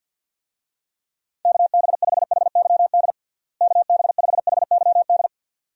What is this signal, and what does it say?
Morse code, heard as a keyed tone, about 47 wpm, 700 Hz. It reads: KB5HYD KB5HYD
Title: USA Call Signs - 6 Characters - Repeated 2x - 50wpm